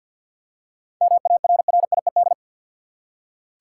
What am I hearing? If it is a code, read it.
Morse code, heard as a keyed tone, about 48 wpm, 700 Hz. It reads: MARRIED
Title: Seven Letter Words - 50wpm